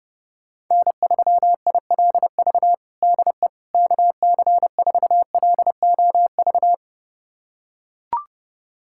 Morse code, heard as a keyed tone, about 32 wpm, 700 Hz. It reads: N3ILV DE KC4LOV
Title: USA Callsign DE Callsign 30wpm